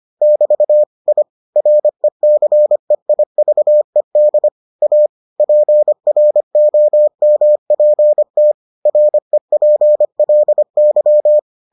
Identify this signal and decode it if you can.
Morse code, heard as a keyed tone, about 25 wpm, 600 Hz. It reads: <BT> I RECEIVED A PROMPT REPLY